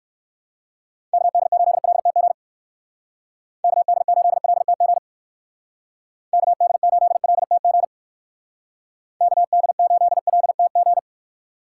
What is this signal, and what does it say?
Morse code, heard as a keyed tone, about 59 wpm, 700 Hz. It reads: KD8LTG KD8LTG KD8LTG KD8LTG